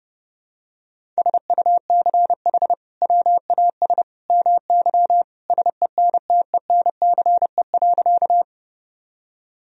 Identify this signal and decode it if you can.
Morse code, heard as a keyed tone, about 30 wpm, 700 Hz. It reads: SUCH WAS MY SENTENCE.
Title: Sentences from Top 1000 Words - Review Part 1 - 30wpm